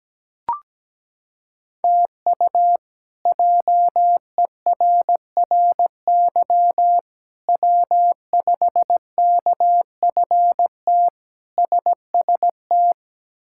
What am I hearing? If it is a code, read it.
Morse code, heard as a keyed tone, about 17 wpm, 700 Hz. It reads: TU JERRY W5KFT SST
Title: K1USN SST QSO Practice - 17wpm